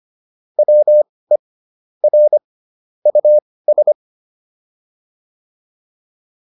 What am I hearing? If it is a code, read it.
Morse code, heard as a keyed tone, about 25 wpm, spaced as for 18 wpm, 600 Hz. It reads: WE R US